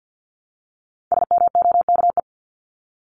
Morse code, heard as a keyed tone, about 50 wpm, 700 Hz. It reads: SMOKE
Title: Top 1501-1750 Words 50wpm